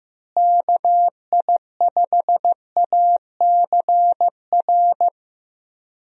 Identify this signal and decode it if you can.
Morse code, heard as a keyed tone, about 15 wpm, 700 Hz. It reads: KI5ACR
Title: USA Call Signs - 6 Characters - Repeated 2x - 15wpm